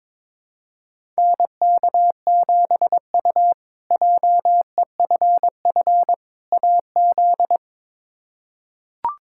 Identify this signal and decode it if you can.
Morse code, heard as a keyed tone, about 22 wpm, 700 Hz. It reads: NK7U JEFF AZ
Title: K1USN SST QSO Practice - 22wpm